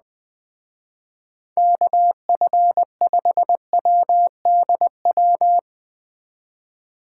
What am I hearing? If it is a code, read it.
Morse code, heard as a keyed tone, about 20 wpm, 700 Hz. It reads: KF5WDW
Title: Speed Racing - USA 2x3 Call Signs 17wpm